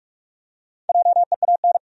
Morse code, heard as a keyed tone, about 47 wpm, 700 Hz. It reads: JEAN
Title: USA Common Names - Max 4 Letters - 45wpm